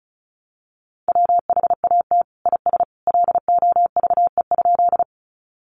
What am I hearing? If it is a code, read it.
Morse code, heard as a keyed tone, about 34 wpm, 700 Hz. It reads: WHAT IS LOVE?
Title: Sentences from Top 400 Words 35wpm